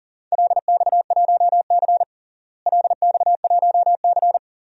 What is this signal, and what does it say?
Morse code, heard as a keyed tone, about 40 wpm, 700 Hz. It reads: LX1C LX1C
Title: DXCC-100 Call Signs - 4 Characters - Repeated 2x - 40wpm